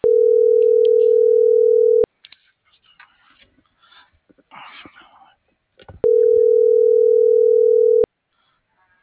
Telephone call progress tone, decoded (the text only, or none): ringback tone